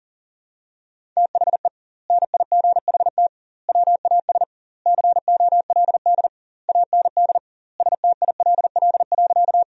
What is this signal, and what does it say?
Morse code, heard as a keyed tone, about 40 wpm, 700 Hz. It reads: THE NIGHT WAS COLD AND STILL.
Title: 2x Word-Spacing - Sentences from Top 400 Words - 40wpm